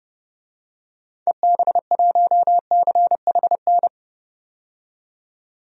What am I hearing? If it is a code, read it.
Morse code, heard as a keyed tone, about 30 wpm, 700 Hz. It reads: EB1CHN
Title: DXCC-100 Call Signs - 6 Characters - 30wpm